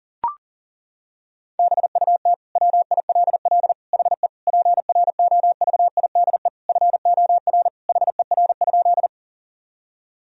Morse code, heard as a keyed tone, about 37 wpm, 700 Hz. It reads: BUT WILL HE PROVIDE FOR HER?
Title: Sentences from Top 1000 Words - Review Part 3 - 40wpm